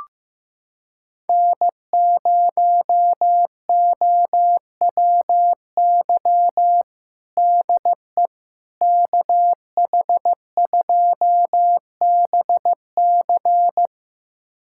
Morse code, heard as a keyed tone, about 15 wpm, 700 Hz. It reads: N0OWY DE KH2BC
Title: USA Callsign DE Callsign - 5 Characters - 15wpm